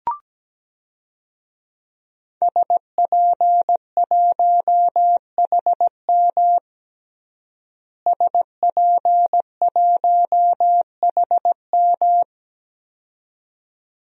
Morse code, heard as a keyed tone, about 17 wpm, 700 Hz. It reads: SP1HM SP1HM
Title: DXCC-100 Call Signs - 5 Characters - Repeated 2x - 17wpm